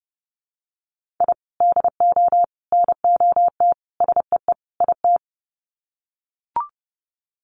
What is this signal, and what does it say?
Morse code, heard as a keyed tone, about 30 wpm, 700 Hz. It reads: I DO NOT SEE IT
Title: Sets of 5 Words Encore2 30wpm